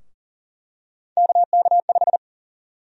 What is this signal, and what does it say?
Morse code, heard as a keyed tone, about 40 wpm, 700 Hz. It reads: KK5
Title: USA Call Sign Prefixes - Repeated 2x - 40wpm